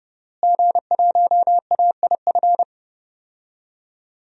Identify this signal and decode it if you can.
Morse code, heard as a keyed tone, about 32 wpm, 700 Hz. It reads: G1AIF
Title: DXCC-100 Call Signs - 5 Characters - Repeated 3x - 30wpm